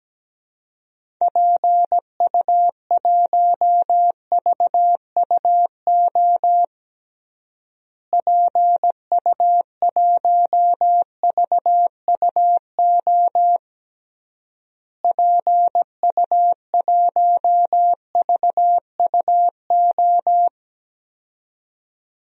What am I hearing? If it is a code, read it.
Morse code, heard as a keyed tone, about 17 wpm, 700 Hz. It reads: PU1VUO PU1VUO PU1VUO
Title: DXCC-100 Call Signs - 6 Characters - Repeated 3x - 17wpm